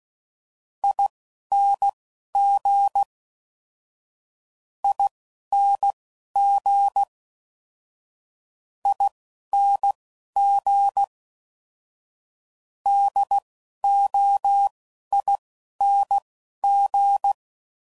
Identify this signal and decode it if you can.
Morse code, heard as a keyed tone, about 16 wpm, spaced as for 11 wpm, 800 Hz. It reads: ING ING ING DOING
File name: Suffixes / Special INGs / Single INGs